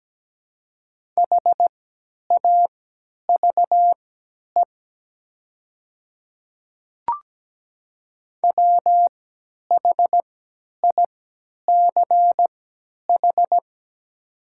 Morse code, heard as a keyed tone, about 17 wpm, spaced as for 10 wpm, 700 Hz. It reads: HAVE WHICH
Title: 3x Character Spacing - Top 100 Words - 17wpm